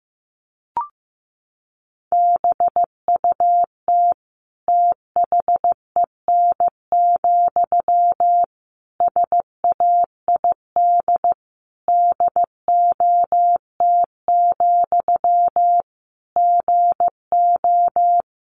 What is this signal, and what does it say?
Morse code, heard as a keyed tone, about 15 wpm, 700 Hz. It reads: BUT THEN, SAID DOT, GO
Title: Sentences from Top 1500 Words 15wpm